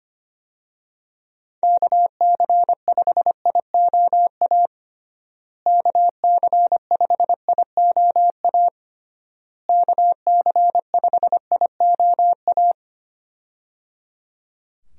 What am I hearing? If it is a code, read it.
Morse code, heard as a keyed tone, about 25 wpm, 700 Hz. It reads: KC5IOA KC5IOA KC5IOA
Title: USA Call Signs - 6 Characters - Repeated 3x - 25wpm